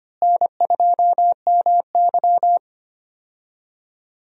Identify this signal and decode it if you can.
Morse code, heard as a keyed tone, about 25 wpm, 700 Hz. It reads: N2MY
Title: USA Call Signs - 4 Characters - Repeated 2x - 25wpm